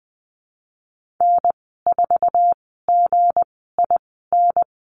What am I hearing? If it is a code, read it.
Morse code, heard as a keyed tone, about 20 wpm, spaced as for 15 wpm, 700 Hz. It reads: N4GIN
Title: 173 Lesson 28g - Number 9 - Call Signs 20wpm